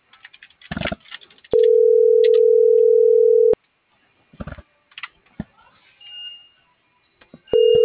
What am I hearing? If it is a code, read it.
Ringback tone (North America), 440 + 480 Hz.